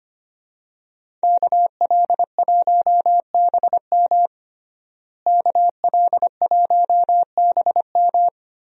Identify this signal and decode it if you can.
Morse code, heard as a keyed tone, about 25 wpm, 700 Hz. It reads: KL1BM KL1BM
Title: USA Call Signs - 5 Characters - Repeated 3x - 25wpm